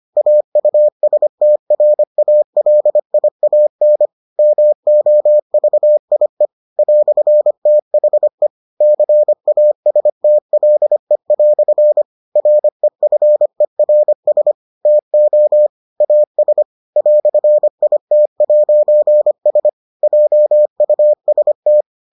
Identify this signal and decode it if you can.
Morse code, heard as a keyed tone, about 25 wpm, 600 Hz. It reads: AUSTRALIAN MOVIE "THE CASTLE" REFERS TO AS "IT'S JUST